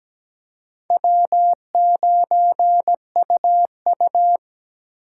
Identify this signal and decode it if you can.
Morse code, heard as a keyed tone, about 17 wpm, 700 Hz. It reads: W9UU
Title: USA Call Signs - 3x Character Spacing - 17wpm